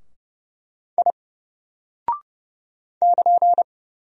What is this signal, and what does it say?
Morse code, heard as a keyed tone, about 30 wpm, 700 Hz. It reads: I <KN>
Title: Single Character (Letter:Number:Punctuation:Prosign) Alt 30wpm